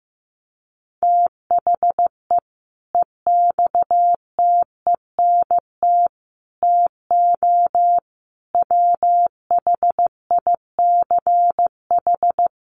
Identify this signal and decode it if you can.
Morse code, heard as a keyed tone, about 15 wpm, 700 Hz. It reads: THE EXTENT TO WHICH
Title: Sets of 4 Words 15wpm